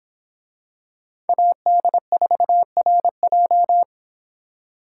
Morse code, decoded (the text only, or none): AD4RJ